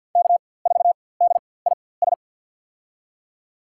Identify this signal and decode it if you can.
Morse code, heard as a keyed tone, about 51 wpm, spaced as for 23 wpm, 700 Hz. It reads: K4DIS
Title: USA Call Signs - 5 Characters - 4x Character Spacing - 50wpm